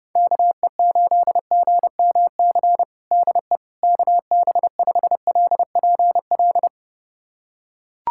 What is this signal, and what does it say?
Morse code, heard as a keyed tone, about 30 wpm, 700 Hz. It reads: KE8GMC DE KB5LPL
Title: USA Callsign DE Callsign - 6 Characters - 30wpm